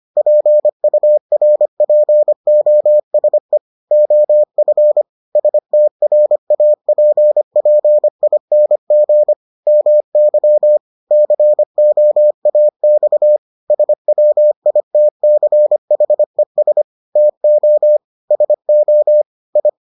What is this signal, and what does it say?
Morse code, heard as a keyed tone, about 25 wpm, 600 Hz. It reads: PURPOSE OF STRAPPING MY COAX SWITCHES TO SO I